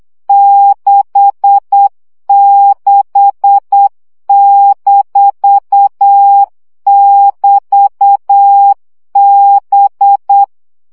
Morse code, heard as a keyed tone, about 8 wpm, 800 Hz. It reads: 66-<BT>B
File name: CW gyakorlas b 6 egyenlo minusz 7 baud